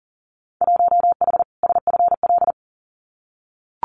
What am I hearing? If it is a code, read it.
Morse code, heard as a keyed tone, about 40 wpm, 700 Hz. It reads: 1H SFL
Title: Winter Field Day Exchanges 40wpm